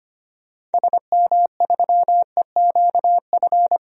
Morse code, heard as a keyed tone, about 25 wpm, 700 Hz. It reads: SM3EQF
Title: DXCC-100 Call Signs - 6 Characters - Repeated 3x - 25wpm